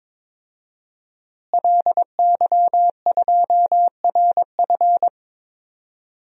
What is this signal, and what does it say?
Morse code, heard as a keyed tone, about 21 wpm, 700 Hz. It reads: LY2RF
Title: DXCC-100 Call Signs - 5 Characters - Repeated 2x - 22wpm